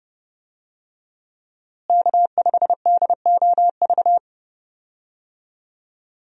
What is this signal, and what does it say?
Morse code, heard as a keyed tone, about 30 wpm, 700 Hz. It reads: K5DOV
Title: USA Call Signs - 5 Characters - Repeated 3x - 30wpm